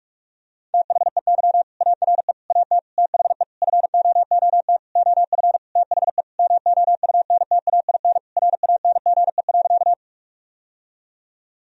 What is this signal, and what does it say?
Morse code, heard as a keyed tone, about 41 wpm, 700 Hz. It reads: THEY ARE AT THE FOOT OF THE MOUNTAIN RANGE.
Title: Sentences from Top 1000 Words - Review Part 4 - 45wpm